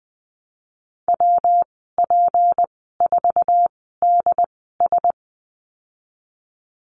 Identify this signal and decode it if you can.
Morse code, heard as a keyed tone, about 20 wpm, spaced as for 15 wpm, 700 Hz. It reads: WP4DS
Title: 231 Lesson 37g - Letter x - Call Signs 20wpm